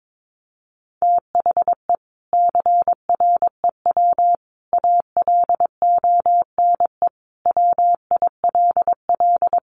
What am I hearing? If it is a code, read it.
Morse code, heard as a keyed tone, about 22 wpm, 700 Hz. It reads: THE CREW ALONE WILL
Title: Sentences from Top 1500 Words 22wpm